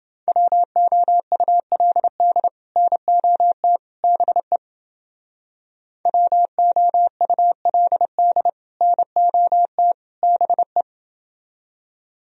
Morse code, heard as a keyed tone, about 30 wpm, 700 Hz. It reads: WOULD NOT BE WOULD NOT BE